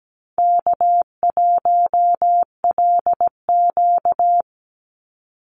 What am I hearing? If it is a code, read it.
Morse code, heard as a keyed tone, about 17 wpm, 700 Hz. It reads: K1LQ
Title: USA Call Signs 17wpm